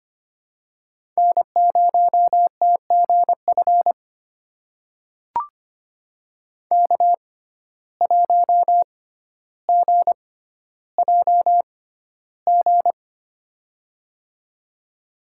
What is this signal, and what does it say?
Morse code, heard as a keyed tone, about 25 wpm, spaced as for 13 wpm, 700 Hz. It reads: N0TGF K1GJG
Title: USA Call Signs - 5 Characters - 6x Character Spacing - 25wpm